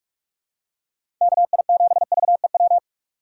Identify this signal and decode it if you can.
Morse code, heard as a keyed tone, about 45 wpm, 700 Hz. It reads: KI7VEW
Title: USA Call Signs - 6 Characters - 6x Character Spacing - 45wpm